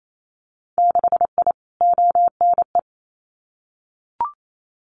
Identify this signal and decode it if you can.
Morse code, heard as a keyed tone, about 28 wpm, 700 Hz. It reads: 6I ONE
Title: Winter Field Day Exchanges Encore 28wpm